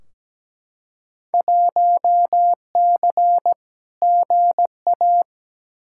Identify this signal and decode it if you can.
Morse code, heard as a keyed tone, about 17 wpm, 700 Hz. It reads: 1C GA
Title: Field Day Exchanges - 17wpm